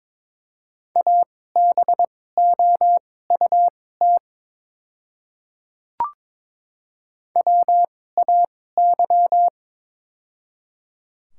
Morse code, heard as a keyed tone, about 22 wpm, spaced as for 16 wpm, 700 Hz. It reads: ABOUT WAY
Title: 2x Character Spacing - Top 500 Words - 22wpm